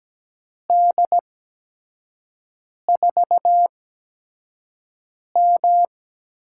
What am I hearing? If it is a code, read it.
Morse code, heard as a keyed tone, about 17 wpm, spaced as for 5 wpm, 700 Hz. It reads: D4M
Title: USA Call Signs - 4 Characters - 8x Character Spacing - 17wpm